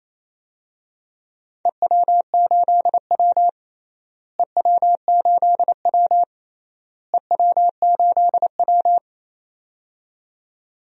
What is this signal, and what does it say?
Morse code, heard as a keyed tone, about 28 wpm, 700 Hz. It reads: EW8W EW8W EW8W
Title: DXCC-100 Call Signs - 4 Characters - Repeated 3x - 28wpm